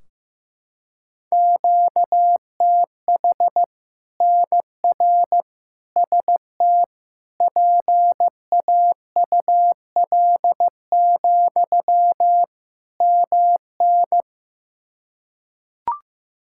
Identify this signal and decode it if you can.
Morse code, heard as a keyed tone, about 15 wpm, 700 Hz. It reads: QTH NR ST PAUL, MN
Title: K1USN SST QSO Practice 15wpm